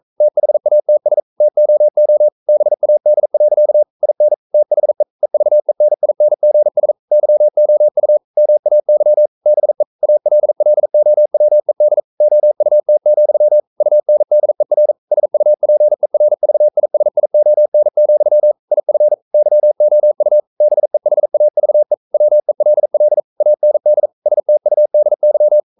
Morse code, heard as a keyed tone, about 42 wpm, 600 Hz. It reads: THATS TOO BAD. IN THE EVENINGS YOU MAY BE ALLOWED OUT, UNDER SUPERVISION, IF YOU BEHAVE WELL AND STUDY
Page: Romanen Never af Ken Follet på engelsk.